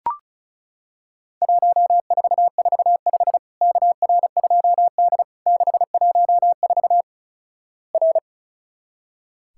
Morse code, heard as a keyed tone, about 36 wpm, 700 Hz. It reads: 1445 KR2D 614 R
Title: Worked All Europe DX Contest - QTC Exchange - 35wpm